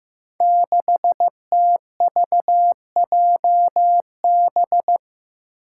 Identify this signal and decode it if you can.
Morse code, heard as a keyed tone, about 15 wpm, 700 Hz. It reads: 6TVJB
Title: Five Challenging Letter-Number Combinations 15wpm